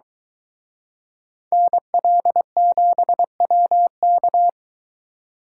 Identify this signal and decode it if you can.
Morse code, heard as a keyed tone, about 23 wpm, 700 Hz. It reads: NL7WK